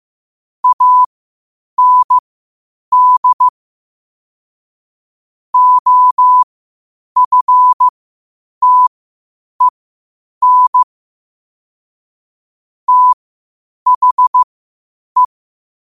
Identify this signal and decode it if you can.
Morse code, heard as a keyed tone, about 15 wpm, spaced as for 8 wpm, 1000 Hz. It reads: AND OFTEN THE
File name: qotdfiles / 8wpm